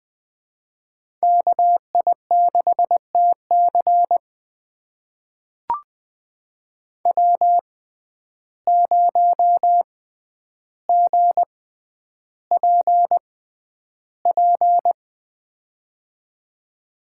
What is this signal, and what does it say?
Morse code, heard as a keyed tone, about 21 wpm, spaced as for 10 wpm, 700 Hz. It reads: KI6TC W0GPP
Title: USA Call Signs - 5 Characters - 6x Character Spacing - 20wpm